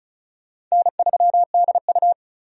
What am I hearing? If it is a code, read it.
Morse code, heard as a keyed tone, about 36 wpm, 700 Hz. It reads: N3DU
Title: USA Call Signs - 4 Characters - 35wpm